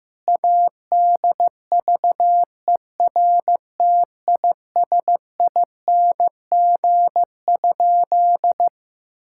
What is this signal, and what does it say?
Morse code, heard as a keyed tone, about 15 wpm, 700 Hz. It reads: ADVERTISING?
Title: Sentences from Top 1000 Words - Review Part 6 - 15wpm